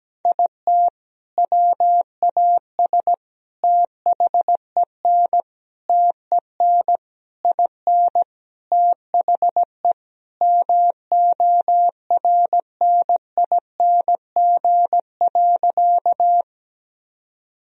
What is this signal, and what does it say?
Morse code, heard as a keyed tone, about 17 wpm, 700 Hz. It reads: IT WAS THEN TEN IN THE MORNING.
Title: Sentences from Top 1000 Words - Review Part 2 - 17wpm